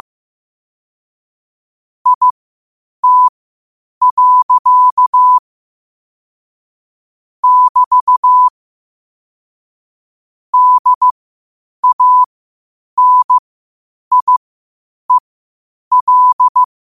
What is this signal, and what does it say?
Morse code, heard as a keyed tone, about 15 wpm, spaced as for 8 wpm, 1000 Hz. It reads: IT. <BT> DANIEL
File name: qotdfiles / 8wpm